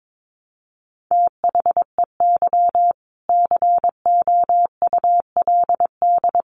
Morse code, heard as a keyed tone, about 22 wpm, 700 Hz. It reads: THEY COULD
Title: Sentences from Top 1200 Words 22wpm